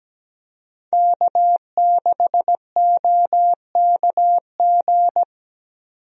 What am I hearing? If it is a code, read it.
Morse code, heard as a keyed tone, about 17 wpm, 700 Hz. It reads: K6OKG
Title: USA Call Signs - 5 Characters - 17wpm